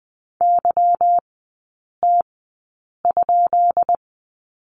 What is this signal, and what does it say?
Morse code, heard as a keyed tone, about 20 wpm, spaced as for 8 wpm, 690 Hz. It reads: YT?
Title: 191 Lesson 31e - Letter v - Three Characters 20wpm